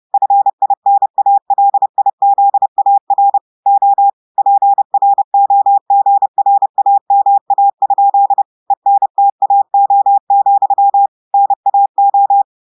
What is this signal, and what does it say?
Morse code, heard as a keyed tone, about 30 wpm, 800 Hz. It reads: FINALIZAR O PROGRAMA? ENTAO, NAO
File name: audios / portugais / portugais texte en clair 2 vitesse 30 mots minutes